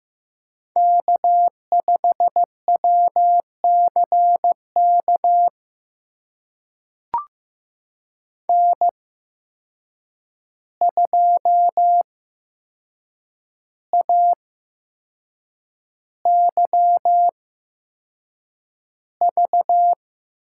Morse code, heard as a keyed tone, about 15 wpm, spaced as for 6 wpm, 700 Hz. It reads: K5WCK N2AYV
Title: USA Call Signs - 8x Character Spacing - 15wpm